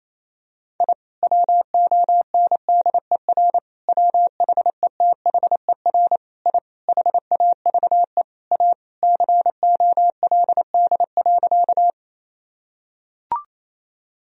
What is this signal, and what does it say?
Morse code, heard as a keyed tone, about 28 wpm, 700 Hz. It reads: I WONDER WHETHER I HAVE A COLD.
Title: Sentences from Top 1000 Words - Review Part 3 - 28wpm